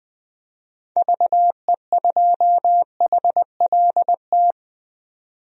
Morse code, heard as a keyed tone, about 20 wpm, 700 Hz. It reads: VE2HLT
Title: DXCC-100 Call Signs - 6 Characters - 20wpm